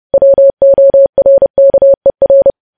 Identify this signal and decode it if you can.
Morse code, heard as a keyed tone, about 30 wpm, 550 Hz. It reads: WORKER
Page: news headlines in morse code at 30 wpm